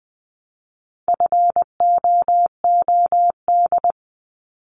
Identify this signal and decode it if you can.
Morse code, heard as a keyed tone, about 20 wpm, 700 Hz. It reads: FOOD
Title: Multi-letter Phonemes 20wpm